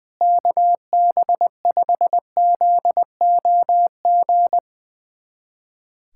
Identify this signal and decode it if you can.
Morse code, heard as a keyed tone, about 21 wpm, 700 Hz. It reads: KB5ZOG
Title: USA Callsign DE Callsign - 6 Characters - 20wpm